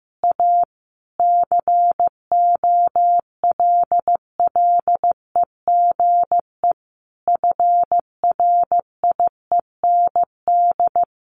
Morse code, heard as a keyed tone, about 15 wpm, 700 Hz. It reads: A COLLEGE FRIEND
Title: Sentences from Top 1300 Words 15wpm